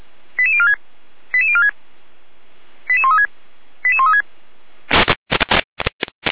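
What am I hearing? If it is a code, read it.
Five-tone selective call (ZVEI), heard as quick runs of five tones, tones of about 70 ms. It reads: 88046 88046 88136 88136
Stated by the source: Five-tone ZVEI selective call 88402 received by SDR.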